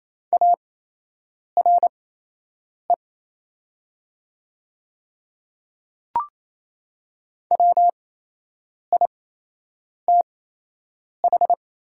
Morse code, heard as a keyed tone, about 28 wpm, spaced as for 8 wpm, 700 Hz. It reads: ARE WITH
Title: 8x Character Spacing - Top 100 Words - 28wpm